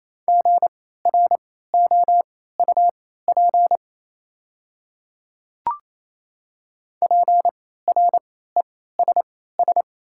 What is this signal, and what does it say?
Morse code, heard as a keyed tone, about 28 wpm, spaced as for 16 wpm, 700 Hz. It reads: GROUP PRESS
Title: 3x Character Spacing - Top 300 Words - 28wpm